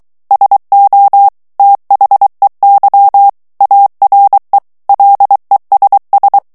Morse code, heard as a keyed tone, about 23 wpm, 800 Hz. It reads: SO THEY ARE LESS
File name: morse mp3 / ANGLAIS texte clair 1 vitesse 22 mots minute